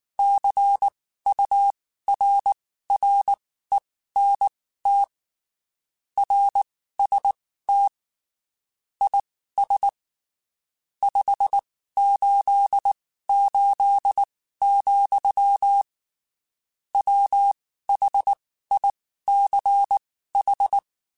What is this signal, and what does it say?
Morse code, heard as a keyed tone, about 19 wpm, spaced as for 14 wpm, 800 Hz. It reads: CURRENT RST IS 588, WHICH
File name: (spaced) / QSOs Spaced-mp3s / WA7QOX 1 @18 sp